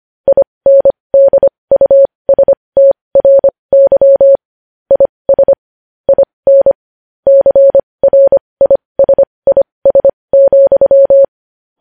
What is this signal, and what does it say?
Morse code, heard as a keyed tone, about 25 wpm, spaced as for 20 wpm, 550 Hz. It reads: INDUSTRY IS IN CRISIS,
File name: podcast / News headlines-1x-25x20